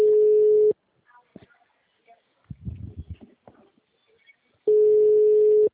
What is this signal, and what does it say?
Ringback tone (Europe, 425 Hz), with speech.